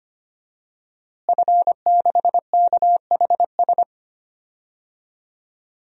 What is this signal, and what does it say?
Morse code, heard as a keyed tone, about 25 wpm, 700 Hz. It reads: F6KHS
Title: DXCC-100 Call Signs - 5 Characters - Repeated 2x - 25wpm